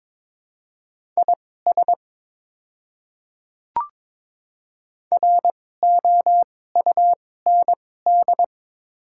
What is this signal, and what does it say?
Morse code, heard as a keyed tone, about 22 wpm, spaced as for 16 wpm, 700 Hz. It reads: IS ROUND
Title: 2x Character Spacing - Top 300 Words - 22wpm